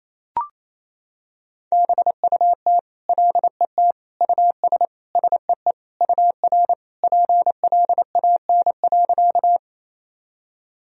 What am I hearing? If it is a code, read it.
Morse code, heard as a keyed tone, about 28 wpm, 700 Hz. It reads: BUT LET US SEE UR PLAN.
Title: Sentences from Top 500 Words and Abbreviations 28wpm